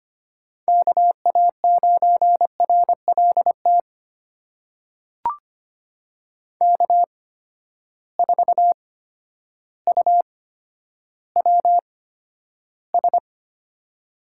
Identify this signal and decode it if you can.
Morse code, heard as a keyed tone, about 25 wpm, spaced as for 11 wpm, 700 Hz. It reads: KA9RLT K4UWS
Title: USA Call Signs - 8x Character Spacing - 25wpm